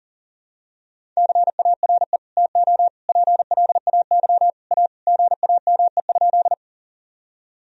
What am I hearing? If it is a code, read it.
Morse code, heard as a keyed tone, about 40 wpm, 700 Hz. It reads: CARE TO PLAY A GAME?
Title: Sentences from Top 1000 Words - Review Part 1 - 40wpm